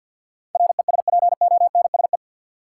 Morse code, heard as a keyed tone, about 51 wpm, 700 Hz. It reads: RESPONSE
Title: Sentences from Top 1400 Words - New Word Drill Alt - 50wpm